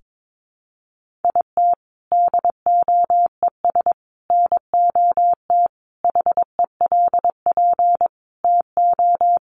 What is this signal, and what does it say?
Morse code, heard as a keyed tone, about 22 wpm, 700 Hz. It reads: IT DOES NOT HELP TO
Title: Sentences from Top 1500 Words 22wpm